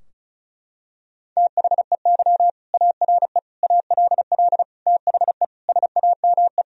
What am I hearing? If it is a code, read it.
Morse code, heard as a keyed tone, about 34 wpm, 700 Hz. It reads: THEY ARE ALL THE SAME
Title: Sentences from Top 1000 Words - Review Part 4 - 35wpm